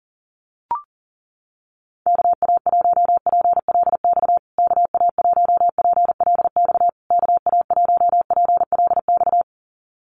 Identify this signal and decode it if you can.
Morse code, heard as a keyed tone, about 40 wpm, 700 Hz. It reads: KA1PLX KA1PLX KA1PLX
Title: USA Call Signs Repeated 3x 40wpm